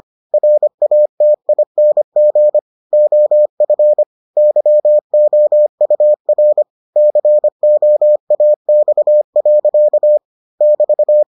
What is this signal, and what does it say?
Morse code, heard as a keyed tone, about 25 wpm, 600 Hz. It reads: RATING OF YOUR COAX. <BT>